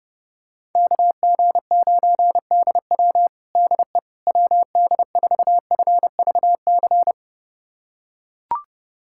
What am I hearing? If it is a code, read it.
Morse code, heard as a keyed tone, about 32 wpm, 700 Hz. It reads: KG9DW DE WD4FVC
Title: USA Callsign DE Callsign 30wpm